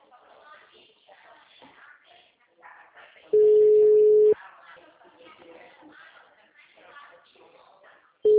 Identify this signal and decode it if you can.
Ringback tone (Europe, 425 Hz), with speech.